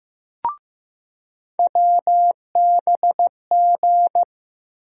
Morse code, heard as a keyed tone, about 15 wpm, 700 Hz. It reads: WBG
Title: USA Call Sign Postfixes - 3 Letters 15wpm